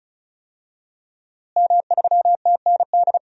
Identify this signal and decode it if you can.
Morse code, heard as a keyed tone, about 36 wpm, 700 Hz. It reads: M3TND
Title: DXCC-100 Call Signs - 5 Characters - Repeated 2x - 35wpm